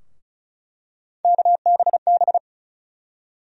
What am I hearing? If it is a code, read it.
Morse code, heard as a keyed tone, about 34 wpm, 700 Hz. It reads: KBB
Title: USA Call Sign Postfixes - 3 Letters 35wpm